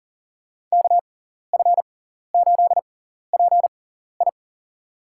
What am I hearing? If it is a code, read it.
Morse code, heard as a keyed tone, about 40 wpm, spaced as for 14 wpm, 700 Hz. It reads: KF8PI
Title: USA Call Signs - 5 Characters - 6x Character Spacing - 40wpm